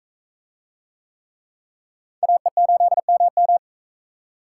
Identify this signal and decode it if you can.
Morse code, heard as a keyed tone, about 42 wpm, 700 Hz. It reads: AE8MM